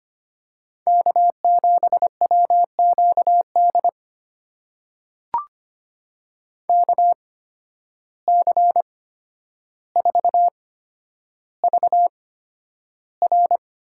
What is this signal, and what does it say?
Morse code, heard as a keyed tone, about 25 wpm, spaced as for 11 wpm, 700 Hz. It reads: K7WQD KC4VR
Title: USA Call Signs - 5 Characters - 8x Character Spacing - 25wpm